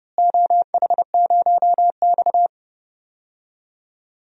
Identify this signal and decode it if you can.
Morse code, heard as a keyed tone, about 31 wpm, 700 Hz. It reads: OH0X
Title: DXCC-100 Call Signs - 4 Characters - Repeated 3x - 30wpm